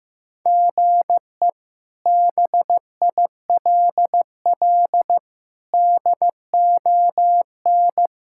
Morse code, heard as a keyed tone, about 15 wpm, 700 Hz. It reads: GE BILL DON
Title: K1USN SST QSO Practice - 15wpm